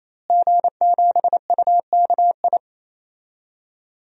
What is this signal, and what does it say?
Morse code, heard as a keyed tone, about 27 wpm, 700 Hz. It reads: G7UKI
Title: DXCC-100 Call Signs - 5 Characters - Repeated 3x - 28wpm